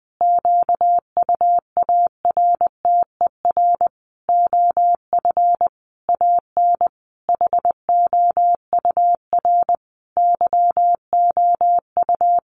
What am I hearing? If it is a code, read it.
Morse code, heard as a keyed tone, about 20 wpm, 700 Hz. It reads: QUARTER OF AN HOUR YOU
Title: Sentences from Top 1300 Words 20wpm